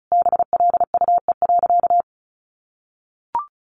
Morse code, heard as a keyed tone, about 34 wpm, 700 Hz. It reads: BLUE.
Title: Additional Easy Phrases 35wpm